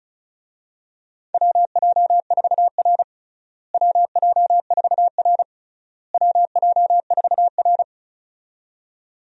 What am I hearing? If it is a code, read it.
Morse code, heard as a keyed tone, about 36 wpm, 700 Hz. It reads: WJ4R WJ4R WJ4R
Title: USA Call Signs - 4 Characters - Repeated 3x - 35wpm